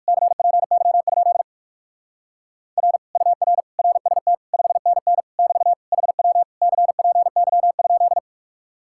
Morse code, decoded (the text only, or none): CPY? R UR RST 5NN <BT> HW CPY?